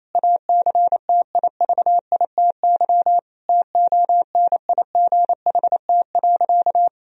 Morse code, heard as a keyed tone, about 28 wpm, 700 Hz. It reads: ACTIVITY TONIGHT.
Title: Sentences from Top 1000 Words - Review Part 4 - 28wpm